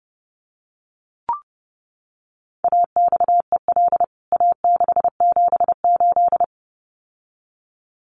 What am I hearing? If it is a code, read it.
Morse code, heard as a keyed tone, about 30 wpm, 700 Hz. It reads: AXEL A678
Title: CWT Exchange 30wpm